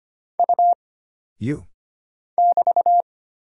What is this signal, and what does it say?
Morse code, heard as a keyed tone, about 25 wpm, 700 Hz. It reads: U <BT>
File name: Single Character (Letter:Number:Punctuation:Prosign) - Mind-Melt - 25wpm